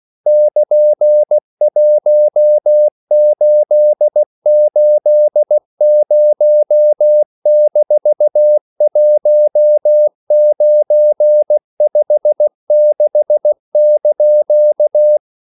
Morse code, heard as a keyed tone, about 16 wpm, 600 Hz. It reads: <KN>1880-1956)